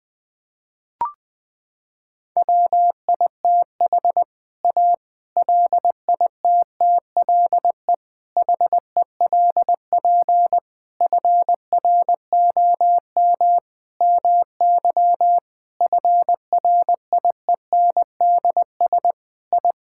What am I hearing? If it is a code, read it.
Morse code, heard as a keyed tone, about 21 wpm, 700 Hz. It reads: WITH A LITTLE HELP FROM MY FRIENDS I
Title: Sentences with 7-Word Phrases - 20wpm